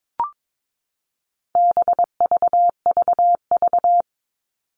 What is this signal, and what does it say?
Morse code, heard as a keyed tone, about 22 wpm, 700 Hz. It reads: BVVV
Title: All Cut Numbers - 4-Digit 22wpm